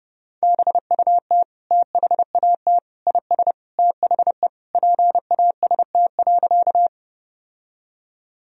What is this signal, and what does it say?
Morse code, heard as a keyed tone, about 31 wpm, 700 Hz. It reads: BUT THAT IS THE PAST.
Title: Sentences from Top 1000 Words - Review Part 3 - 30wpm